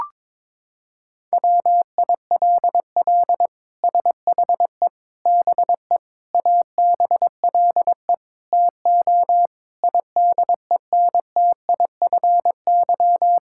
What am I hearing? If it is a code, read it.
Morse code, heard as a keyed tone, about 22 wpm, 700 Hz. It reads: WILL SHE BE ABLE TO IDENTIFY
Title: Sentences from Top 1000 Words - Review Part 6 - 22wpm